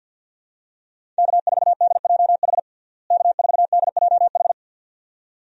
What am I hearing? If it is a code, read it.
Morse code, heard as a keyed tone, about 47 wpm, 700 Hz. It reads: K4DJH K4DJH
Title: USA Call Signs - 5 Characters - Repeated 2x - 50wpm